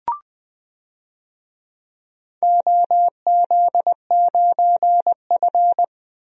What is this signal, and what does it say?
Morse code, heard as a keyed tone, about 20 wpm, 700 Hz. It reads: OZ9F
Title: DXCC-100 Call Signs - 4 Characters - 20wpm